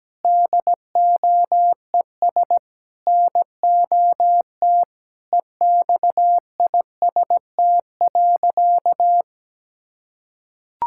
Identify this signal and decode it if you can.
Morse code, heard as a keyed tone, about 17 wpm, 700 Hz. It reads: DOES NOT EXIST.
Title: Sentences from Top 1000 Words - Review Part 5 - 17wpm